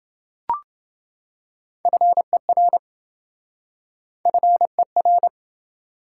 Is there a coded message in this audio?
Yes: Morse code from 1.9 to 5.3 s, heard as a keyed tone, about 26 wpm, 700 Hz. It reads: FER FER